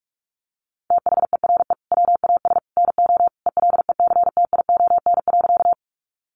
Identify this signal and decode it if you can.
Morse code, heard as a keyed tone, about 44 wpm, 700 Hz. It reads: THERE WAS NO ELECTION.
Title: Sentences from Top 1500 Words 45wpm